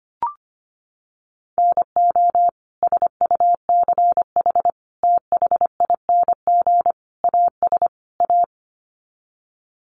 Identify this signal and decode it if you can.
Morse code, heard as a keyed tone, about 25 wpm, 700 Hz. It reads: NO SUCH THING AS A
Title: Sets of 5 Words Encore 25wpm